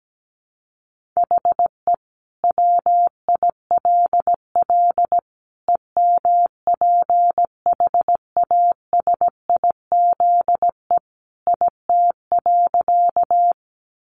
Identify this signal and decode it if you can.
Morse code, heard as a keyed tone, about 17 wpm, 700 Hz. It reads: HE WILL EMPHASIZE IT.
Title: Sentences from Top 1500 Words 17wpm